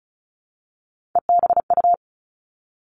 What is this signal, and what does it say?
Morse code, heard as a keyed tone, about 36 wpm, 700 Hz. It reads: EBU
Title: Three Challenging Letter-Number Combinations 35wpm